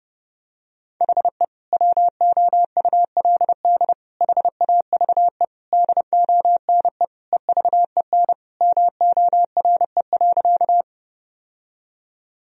Morse code, heard as a keyed tone, about 32 wpm, 700 Hz. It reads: HE WOULD HAVE DONE EVEN MORE.
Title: Sentences from Top 1000 Words - Review Part 1 - 30wpm